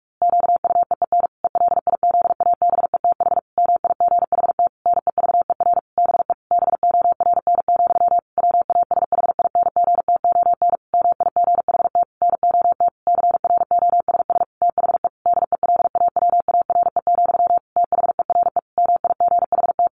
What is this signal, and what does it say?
Morse code, heard as a keyed tone, about 44 wpm, 700 Hz. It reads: QUEEN ELIZABETH MIGHT NEVER BE BORN, WASHINGTON MIGHT NOT CROSS THE DELAWARE, THERE MIGHT
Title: A Sound Of Thunder 45wpm